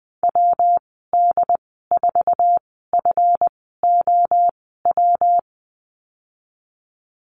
Morse code, heard as a keyed tone, about 20 wpm, spaced as for 15 wpm, 700 Hz. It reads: WD4FOW
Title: 161 Lesson 26g - Letter g - Call Signs 20wpm